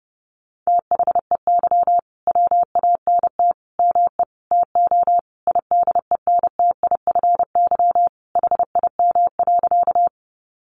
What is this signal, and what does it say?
Morse code, heard as a keyed tone, about 30 wpm, 700 Hz. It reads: THEY WANT ME TO IDENTIFY HIM.
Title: Sentences from Top 900 Words 30wpm